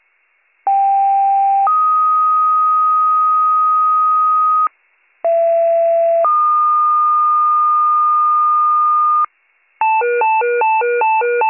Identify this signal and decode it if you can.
Two-tone page, heard as a one-second tone, then a three-second tone, about 790 then 1250 Hz, about 670 then 1220 Hz. It is heard from 0.7 to 9.3 s.